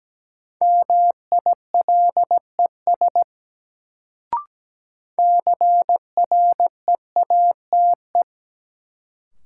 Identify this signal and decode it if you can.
Morse code, heard as a keyed tone, about 17 wpm, 700 Hz. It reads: MILES CREATE
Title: Sentences from Top 1000 Words - Review Part 3 - New Word Drill - 17wpm